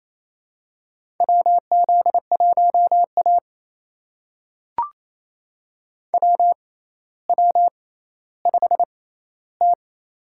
Morse code, heard as a keyed tone, about 28 wpm, spaced as for 14 wpm, 700 Hz. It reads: WZ1A WW5T
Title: USA Call Signs - 6x Character Spacing - 28wpm